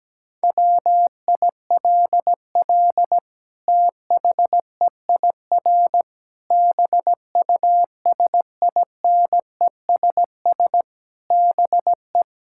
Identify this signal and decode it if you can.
Morse code, heard as a keyed tone, about 17 wpm, 700 Hz. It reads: WILL THEIR BUSINESS BE
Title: Sentences from Top 1000 Words - Review Part 5 - 17wpm